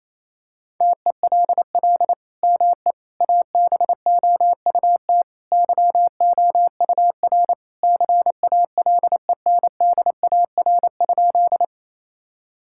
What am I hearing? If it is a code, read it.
Morse code, heard as a keyed tone, about 28 wpm, 700 Hz. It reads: TELL ME ABOUT YOUR CALENDAR?
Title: Sentences from Top 1600 Words 28wpm